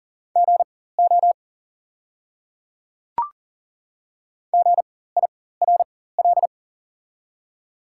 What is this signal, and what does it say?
Morse code, heard as a keyed tone, about 40 wpm, spaced as for 19 wpm, 690 Hz. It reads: GO GIRL
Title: 4x Character Spacing - Top 500 Words - 40wpm